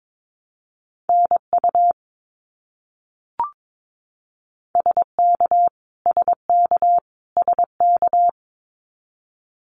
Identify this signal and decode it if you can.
Morse code, heard as a keyed tone, about 22 wpm, 700 Hz. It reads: NU SK SK SK
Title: Canadian Provinces and Territories - Repeated 3x - 22wpm